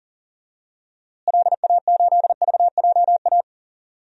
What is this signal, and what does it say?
Morse code, heard as a keyed tone, about 37 wpm, 700 Hz. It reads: LA8VJA
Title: DXCC-100 Call Signs - 6 Characters - Repeated 3x - 40wpm